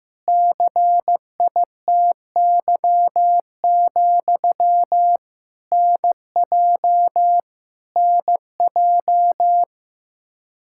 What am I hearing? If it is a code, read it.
Morse code, heard as a keyed tone, about 15 wpm, 700 Hz. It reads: CITY, NJ NJ
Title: Practice QSO - Exchange 1 - Beginning - 15wpm